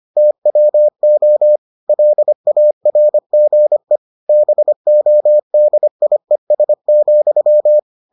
Morse code, heard as a keyed tone, about 25 wpm, 600 Hz. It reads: TWO LARGE BODIES,